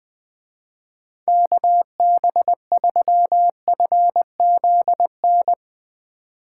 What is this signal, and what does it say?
Morse code, heard as a keyed tone, about 21 wpm, 700 Hz. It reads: KB3FZN
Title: Speed Racing - USA 2x3 Call Signs 17wpm